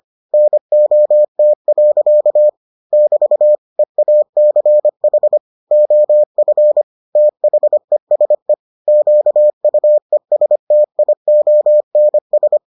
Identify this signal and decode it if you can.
Morse code, heard as a keyed tone, about 25 wpm, 600 Hz. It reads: NOT. <BT> EACH OF THESE QUESTIONS